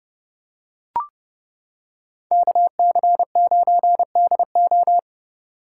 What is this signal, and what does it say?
Morse code, heard as a keyed tone, about 28 wpm, 700 Hz. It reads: KC9DO